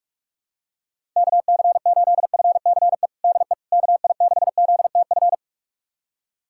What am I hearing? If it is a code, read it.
Morse code, heard as a keyed tone, about 46 wpm, 700 Hz. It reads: KC8FCE DE KI6ZTF
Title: USA Callsign DE Callsign - 6 Characters - 45wpm